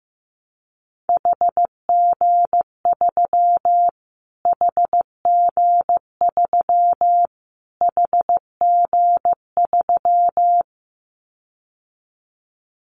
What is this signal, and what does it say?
Morse code, heard as a keyed tone, about 15 wpm, 700 Hz. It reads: HG3 HG3 HG3
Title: Callsign Prefixes for Top 25 DXCC - Repeated 3x - 15wpm